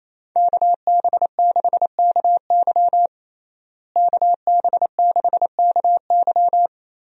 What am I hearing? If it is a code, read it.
Morse code, heard as a keyed tone, about 28 wpm, 700 Hz. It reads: KB6KY KB6KY
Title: USA Call Signs - 5 Characters - Repeated 3x - 28wpm